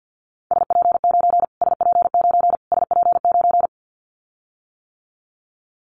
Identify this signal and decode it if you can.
Morse code, heard as a keyed tone, about 50 wpm, 700 Hz. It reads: SP9 SP9 SP9
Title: Callsign Prefixes for Top 25 DXCC - Repeated 3x - 50wpm